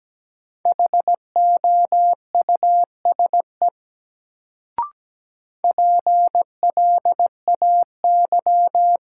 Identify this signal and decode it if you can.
Morse code, heard as a keyed tone, about 17 wpm, 700 Hz. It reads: HOUSE PLAY
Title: Sentences from Top 200 Words - New Word Drill - 17wpm